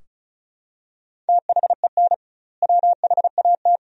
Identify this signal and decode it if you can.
Morse code, heard as a keyed tone, about 36 wpm, 700 Hz. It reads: THEN WHAT
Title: 2x Word-Spacing - Sentences from Top 400 Words - 35wpm